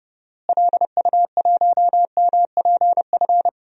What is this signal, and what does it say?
Morse code, heard as a keyed tone, about 30 wpm, 700 Hz. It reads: LU1MPF
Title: DXCC-100 Call Signs - 6 Characters - Repeated 2x - 30wpm